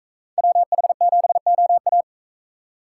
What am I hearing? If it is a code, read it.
Morse code, heard as a keyed tone, about 41 wpm, 700 Hz. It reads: WH7OA